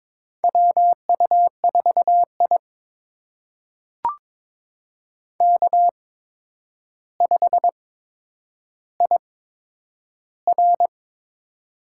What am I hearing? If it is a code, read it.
Morse code, heard as a keyed tone, about 22 wpm, spaced as for 9 wpm, 700 Hz. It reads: WU4I K5IR
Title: USA Call Signs - 4 Characters - 8x Character Spacing - 22wpm